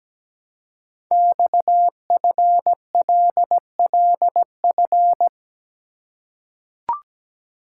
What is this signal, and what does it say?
Morse code, heard as a keyed tone, about 17 wpm, 700 Hz. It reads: XFLLF
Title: Code Groups - 5 Letters - 17wpm